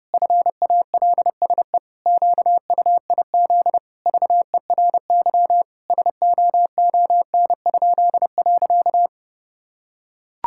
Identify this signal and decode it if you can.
Morse code, heard as a keyed tone, about 32 wpm, 700 Hz. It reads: FALSE QUIZ VERY SOON?.
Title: Pangram Challenge v2 - 30wpm